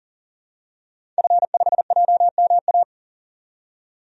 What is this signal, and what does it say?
Morse code, heard as a keyed tone, about 40 wpm, 700 Hz. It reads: F5JMA
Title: DXCC-100 Call Signs - 5 Characters - 40wpm